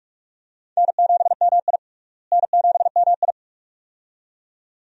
Morse code, heard as a keyed tone, about 41 wpm, 700 Hz. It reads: N7MI N7MI
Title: USA Call Signs - 4 Characters - Repeated 2x - 45wpm